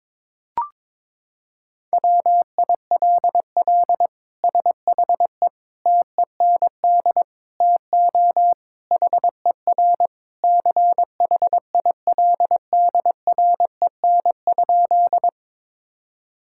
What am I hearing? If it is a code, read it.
Morse code, heard as a keyed tone, about 22 wpm, 700 Hz. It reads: WILL SHE TEND TO HER CHILDREN?
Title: Sentences from Top 1000 Words - Review Part 5 - 22wpm